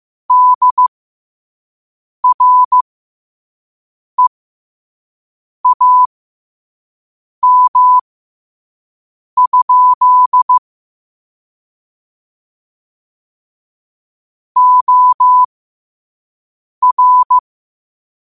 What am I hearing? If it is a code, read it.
Morse code, heard as a keyed tone, about 15 wpm, spaced as for 5 wpm, 1000 Hz. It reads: DREAM? OR
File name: qotdfiles / 5wpm